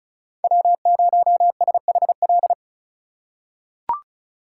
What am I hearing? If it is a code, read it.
Morse code, heard as a keyed tone, about 36 wpm, 700 Hz. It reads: W0SHL
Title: K1USN SST QSO Practice - 35wpm